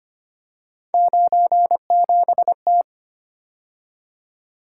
Morse code, heard as a keyed tone, about 29 wpm, 700 Hz. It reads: 97T
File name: Speed-Racing - Three Letter-Number Combinations - 25wpm